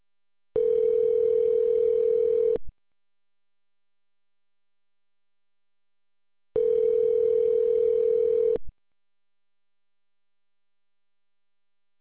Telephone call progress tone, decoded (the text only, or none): ringback tone